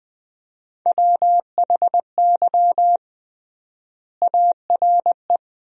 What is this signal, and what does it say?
Morse code, heard as a keyed tone, about 20 wpm, 700 Hz. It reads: WHY ARE
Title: 3x Word-Spacing - Sentences from Top 200 Words - 20wpm